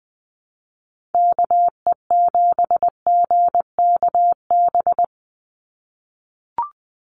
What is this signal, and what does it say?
Morse code, heard as a keyed tone, about 20 wpm, 700 Hz. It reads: KE7GKB
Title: USA Call Signs 20wpm